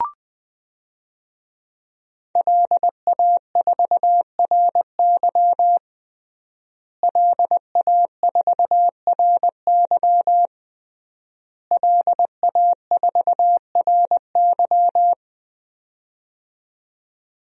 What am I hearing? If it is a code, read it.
Morse code, heard as a keyed tone, about 20 wpm, 700 Hz. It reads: LA4RY LA4RY LA4RY
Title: DXCC-100 Call Signs - 5 Characters - Repeated 3x - 20wpm